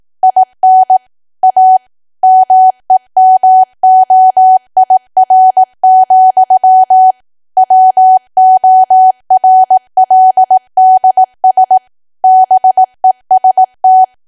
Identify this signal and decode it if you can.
Morse code, heard as a keyed tone, about 18 wpm, 750 Hz. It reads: IN A MEMOIR, WORLDS BEST
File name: Archive / 18 WPM / 18WPM